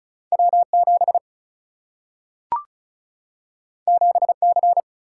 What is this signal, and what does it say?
Morse code, heard as a keyed tone, about 36 wpm, 700 Hz. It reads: W7 7C
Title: Single-Letter Single-Number 35wpm